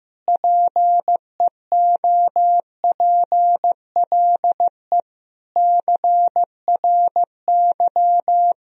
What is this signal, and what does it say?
Morse code, heard as a keyed tone, about 15 wpm, 700 Hz. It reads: PEOPLE CRY
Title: Sentences from Top 1000 Words - Review Part 2 - 15wpm